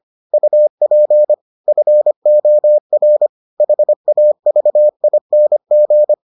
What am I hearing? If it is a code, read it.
Morse code, heard as a keyed tone, about 25 wpm, 600 Hz. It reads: UP FOR HAVING